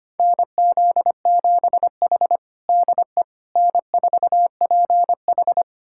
Morse code, heard as a keyed tone, about 25 wpm, 700 Hz. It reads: NZ7H DE N4PH
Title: USA Callsign DE Callsign - 4 Characters - 25wpm